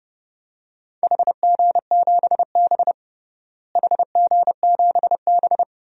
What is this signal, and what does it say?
Morse code, heard as a keyed tone, about 30 wpm, 700 Hz. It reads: HG7B HG7B
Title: DXCC-100 Call Signs - 4 Characters - Repeated 2x - 30wpm